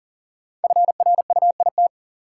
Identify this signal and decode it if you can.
Morse code, heard as a keyed tone, about 40 wpm, 700 Hz. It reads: FRUIT
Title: Sentences from Top 1300 Words - New Word Drill - 40wpm